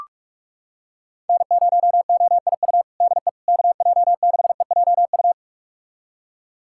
Morse code, heard as a keyed tone, about 46 wpm, 700 Hz. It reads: N0OIU DE KJ6EJU
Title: USA Callsign DE Callsign 45wpm